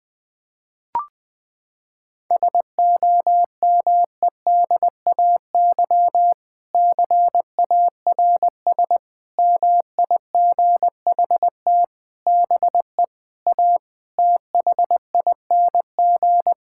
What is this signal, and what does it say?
Morse code, heard as a keyed tone, about 20 wpm, 700 Hz. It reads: SOMEDAY CARS MIGHT BE A THING
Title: Sentences with 6-Word Phrases - 20wpm